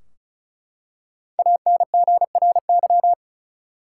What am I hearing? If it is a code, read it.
Morse code, heard as a keyed tone, about 36 wpm, 700 Hz. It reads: ANGRY
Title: Five Letter Words - 35wpm